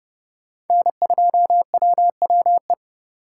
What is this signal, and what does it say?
Morse code, heard as a keyed tone, about 29 wpm, 700 Hz. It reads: N2WWE
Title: USA Call Signs - 5 Characters - 2x Character Spacing - 30wpm